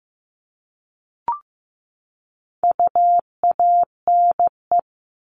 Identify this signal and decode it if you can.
Morse code, heard as a keyed tone, about 15 wpm, 700 Hz. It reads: UANE
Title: All Cut Numbers - 4-Digit 15wpm